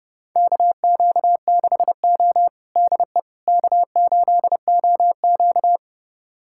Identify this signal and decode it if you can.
Morse code, heard as a keyed tone, about 30 wpm, 700 Hz. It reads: KQ6O DE K8OQ
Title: USA Callsign DE Callsign - 4 Characters - 30wpm